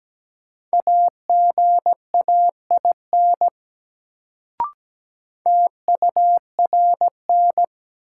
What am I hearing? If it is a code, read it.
Morse code, heard as a keyed tone, about 17 wpm, 700 Hz. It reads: AGAIN TURN
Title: Sets of 4 Words - New Word Drill - 17wpm